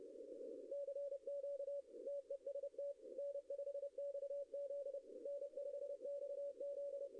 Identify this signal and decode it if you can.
Morse code, heard as a keyed tone, about 29 wpm, 570 Hz. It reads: CQ TEST N5XZ N5XZ
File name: contests / arrl-160-k5tr-op-w5tn -0900